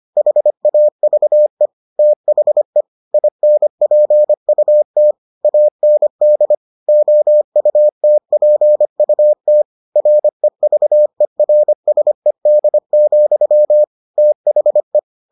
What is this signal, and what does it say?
Morse code, heard as a keyed tone, about 25 wpm, 600 Hz. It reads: HAVE THE INPUT AND OUTPUT REVERSED, THE